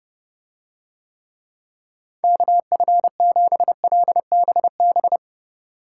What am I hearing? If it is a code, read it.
Morse code, heard as a keyed tone, about 29 wpm, 700 Hz. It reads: KF7LBB